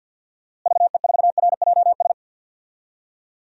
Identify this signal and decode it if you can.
Morse code, heard as a keyed tone, about 47 wpm, 700 Hz. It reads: VE4RJS
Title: DXCC-100 Call Signs - 6 Characters - Repeated 3x - 50wpm